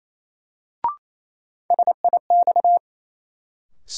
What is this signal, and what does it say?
Morse code, heard as a keyed tone, about 28 wpm, 700 Hz. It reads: SIX
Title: Sentences from Top 600 Words - New Word Drill - 28wpm